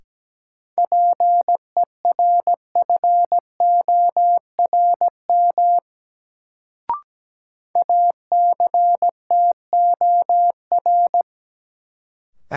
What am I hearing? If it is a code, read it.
Morse code, heard as a keyed tone, about 17 wpm, 700 Hz. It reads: PERFORM ACTOR
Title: Sentences from Top 1500 Words - New Word Drill Alt - 17wpm